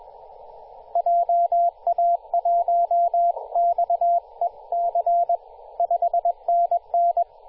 Morse code, heard as a keyed tone, about 21 wpm, 690 Hz, with radio noise. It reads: JA1XEC 5NN
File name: t2r-18-071mhz-cw